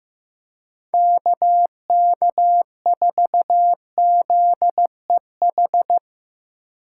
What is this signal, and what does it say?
Morse code, heard as a keyed tone, about 15 wpm, 700 Hz. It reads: KK4ZEH
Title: USA Call Signs - 6 Characters - 6x Character Spacing - 15wpm